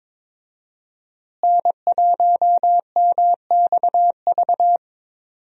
Morse code, heard as a keyed tone, about 22 wpm, 700 Hz. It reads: N1MXV
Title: USA Call Signs - 5 Characters - Repeated 3x - 22wpm